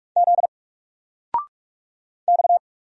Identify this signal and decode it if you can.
Morse code, heard as a keyed tone, about 44 wpm, 700 Hz. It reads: Z X
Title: Single Character (Letter:Number:Punctuation:Prosign) 45wpm